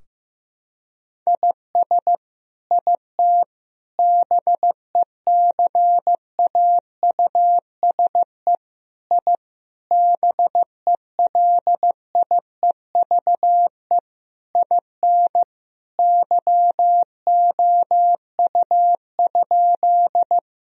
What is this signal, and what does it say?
Morse code, heard as a keyed tone, about 15 wpm, 700 Hz. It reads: IS IT BECAUSE I BELIEVE IN YOU?
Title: Sentences from Top 1000 Words - Review Part 3 - 15wpm